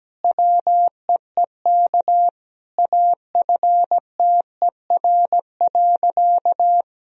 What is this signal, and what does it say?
Morse code, heard as a keyed tone, about 17 wpm, 700 Hz. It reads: WEEK AFTER.
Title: Sentences from Top 500 Words and Abbreviations 17wpm